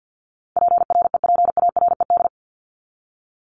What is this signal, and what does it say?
Morse code, heard as a keyed tone, about 50 wpm, 700 Hz. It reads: PREPARED
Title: ED Suffix - Top 100 Words with Base Words 50wpm